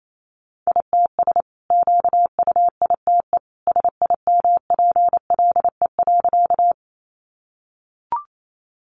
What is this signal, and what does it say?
Morse code, heard as a keyed tone, about 28 wpm, 700 Hz. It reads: ITS QUITE SIMPLE.
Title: Sentences from Top 700 Words 28wpm